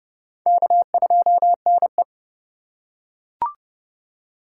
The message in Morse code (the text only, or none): K2NE